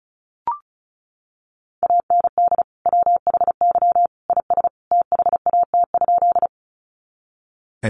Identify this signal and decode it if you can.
Morse code, heard as a keyed tone, about 34 wpm, 700 Hz. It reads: AND WHY IS THAT?
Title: Sentences from Top 200 Words 35wpm